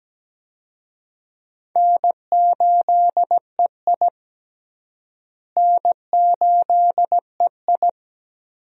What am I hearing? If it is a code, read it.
Morse code, heard as a keyed tone, about 17 wpm, 700 Hz. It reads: N8EI N8EI
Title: USA Call Signs - 4 Characters - Repeated 2x - 17wpm